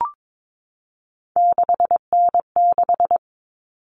Morse code, heard as a keyed tone, about 22 wpm, 700 Hz. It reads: 6N6
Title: All Cut Numbers - 3-Digit 22wpm